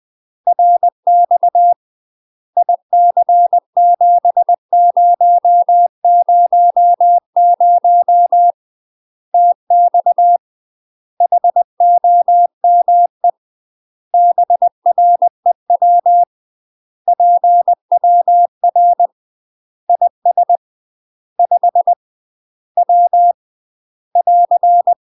Morse code, heard as a keyed tone, about 20 wpm, 700 Hz. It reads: RX IC7000 TX HOME BREW PWR IS 5 W <AR>
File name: UK MASTER QSOs TRAINING / 20WPM / UK MASTER QSOs 0008